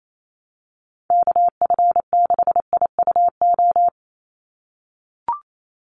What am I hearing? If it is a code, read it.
Morse code, heard as a keyed tone, about 28 wpm, 700 Hz. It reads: KF6IUO
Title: USA Call Signs 28wpm